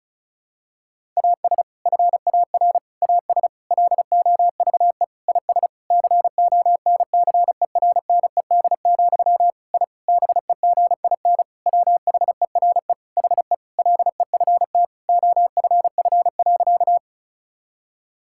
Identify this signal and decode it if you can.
Morse code, heard as a keyed tone, about 36 wpm, 700 Hz. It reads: AS FAR AS LOVE IS CONCERNED, I BEGIN WHERE HE LEFT OFF.
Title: Sentences from Top 1000 Words - Review Part 6 - 35wpm